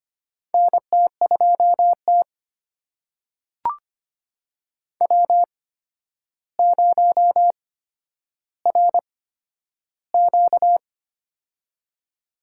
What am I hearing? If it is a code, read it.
Morse code, heard as a keyed tone, about 25 wpm, spaced as for 11 wpm, 700 Hz. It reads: NT2T W0RQ
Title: USA Call Signs - 8x Character Spacing - 25wpm